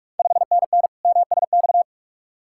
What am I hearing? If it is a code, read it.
Morse code, heard as a keyed tone, about 46 wpm, 690 Hz. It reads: 5NN MSX
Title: Ontario QSO Party Exchange 45wpm